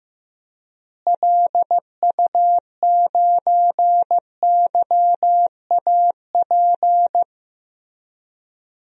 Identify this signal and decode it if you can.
Morse code, heard as a keyed tone, about 15 wpm, 700 Hz. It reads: LU9YAP
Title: DXCC-100 Call Signs - 6 Characters - Repeated 3x - 15wpm